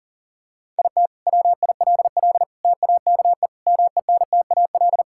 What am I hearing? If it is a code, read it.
Morse code, heard as a keyed tone, about 40 wpm, 700 Hz. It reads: IT WILL TAKE MENTAL
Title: Sentences from Top 1000 Words - Review Part 6 - 40wpm